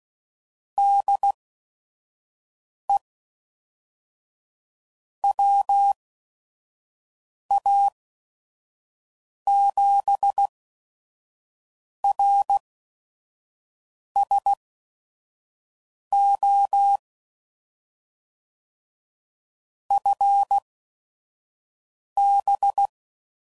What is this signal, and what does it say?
Morse code, heard as a keyed tone, about 16 wpm, spaced as for 5 wpm, 800 Hz. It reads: DE WA7RSO FB
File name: CW Code Groups @5WPM / QSOs / WD0FBT 1